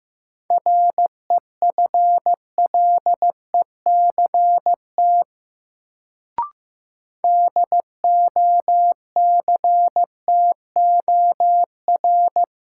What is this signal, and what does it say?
Morse code, heard as a keyed tone, about 16 wpm, 700 Hz. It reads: REFLECT DOCTOR
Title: Sentences from Top 1500 Words - New Word Drill Alt - 15wpm